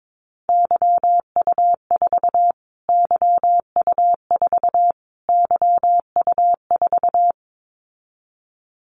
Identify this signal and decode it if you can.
Morse code, heard as a keyed tone, about 22 wpm, 700 Hz. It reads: YU4 YU4 YU4
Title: Callsign Prefixes for Top 25 DXCC - Repeated 3x - 22wpm